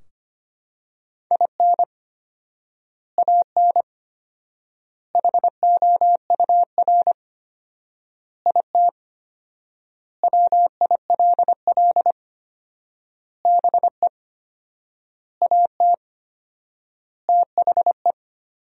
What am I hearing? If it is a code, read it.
Morse code, heard as a keyed tone, about 25 wpm, 700 Hz. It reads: IN AN HOUR IT WILL BE AT THE
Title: 4x Word-Spacing - Sentences from Top 400 Words - 25wpm